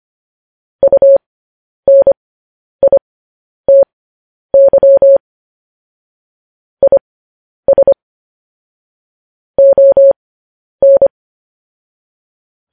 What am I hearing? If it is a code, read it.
Morse code, heard as a keyed tone, about 25 wpm, spaced as for 10 wpm, 550 Hz. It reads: UNITY IS ON
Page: news headlines in morse code at 10 wpm